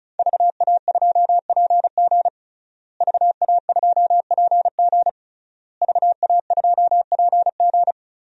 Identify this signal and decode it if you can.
Morse code, heard as a keyed tone, about 36 wpm, 700 Hz. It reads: VA2PG VA2PG VA2PG
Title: DXCC-100 Call Signs - 5 Characters - Repeated 3x - 35wpm